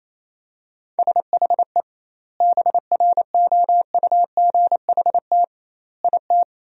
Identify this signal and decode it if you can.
Morse code, heard as a keyed tone, about 28 wpm, 700 Hz. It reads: SHE BROUGHT IT
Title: 2x Word-Spacing - Sentences from Top 500 Words - 28wpm